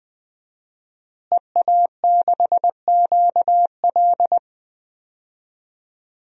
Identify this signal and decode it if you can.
Morse code, heard as a keyed tone, about 20 wpm, 700 Hz. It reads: EA6QL
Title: DXCC-100 Call Signs - 5 Characters - Repeated 2x - 20wpm